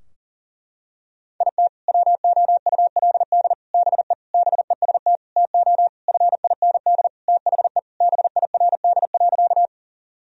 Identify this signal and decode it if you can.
Morse code, heard as a keyed tone, about 40 wpm, 700 Hz. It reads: IT WOULD BE BEST TO FIND THE BIRD.
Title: Sentences from Top 1000 Words - Review Part 2 - 40wpm